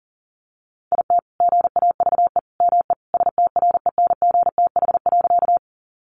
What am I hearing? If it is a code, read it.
Morse code, heard as a keyed tone, about 40 wpm, 700 Hz. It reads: IT GAVE ME STRENGTH.
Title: Sentences from Top 1300 Words 40wpm